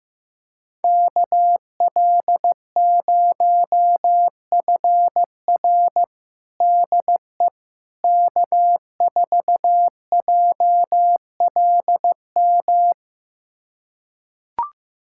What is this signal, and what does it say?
Morse code, heard as a keyed tone, about 15 wpm, 700 Hz. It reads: KL0FR DE K4JLM
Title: USA Callsign DE Callsign - 5 Characters - 15wpm